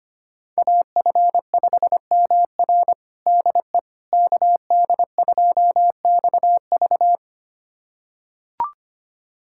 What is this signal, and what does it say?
Morse code, heard as a keyed tone, about 25 wpm, 700 Hz. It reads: AF5MR DE KD2XV
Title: USA Callsign DE Callsign - 5 Characters - 25wpm